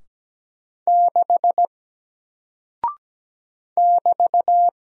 Morse code, heard as a keyed tone, about 17 wpm, 700 Hz. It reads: 6 <BT>
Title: Single Character (Letter:Number:Punctuation:Prosign) Alt 17wpm